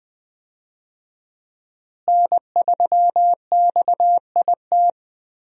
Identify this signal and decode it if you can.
Morse code, heard as a keyed tone, about 20 wpm, 700 Hz. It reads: N3XIT
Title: USA Call Signs - 5 Characters - Repeated 3x - 20wpm